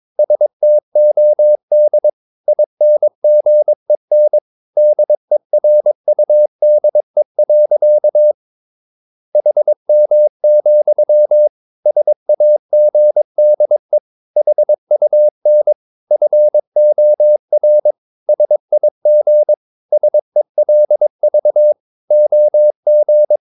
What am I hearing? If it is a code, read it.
Morse code, heard as a keyed tone, about 22 wpm, 600 Hz. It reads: STOD INGEN DERUDE. HM, SAGDE HUN FOR SIG SELV OG
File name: Ferie QRS 22wpm